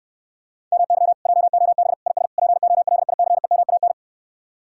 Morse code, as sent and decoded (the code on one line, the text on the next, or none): -- -.-- / .--- --- -... / .. ... / .--. --- .-.. .. -.-. -- .- -.
MY JOB IS POLICEMAN